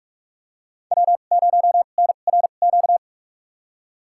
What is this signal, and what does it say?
Morse code, heard as a keyed tone, about 43 wpm, spaced as for 32 wpm, 700 Hz. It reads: W0NRQ
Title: USA Call Signs - 5 Characters - 2x Character Spacing - 45wpm